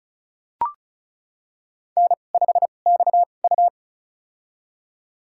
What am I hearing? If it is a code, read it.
Morse code, heard as a keyed tone, about 34 wpm, spaced as for 25 wpm, 700 Hz. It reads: N5XU
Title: USA Call Signs - 4 Characters - 2x Character Spacing - 35wpm